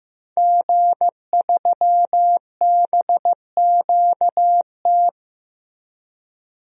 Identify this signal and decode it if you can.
Morse code, heard as a keyed tone, about 15 wpm, 700 Hz. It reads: G3BQT
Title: DXCC-100 Call Signs - 5 Characters - Repeated 3x - 15wpm